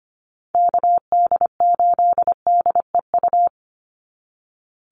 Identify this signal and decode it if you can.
Morse code, heard as a keyed tone, about 25 wpm, 700 Hz. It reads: KD8DEU
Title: NA SOTA Chaser Callsigns 25wpm